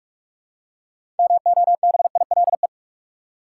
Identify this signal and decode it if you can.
Morse code, heard as a keyed tone, about 44 wpm, 700 Hz. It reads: MOBILE
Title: Six Letter Words - 45wpm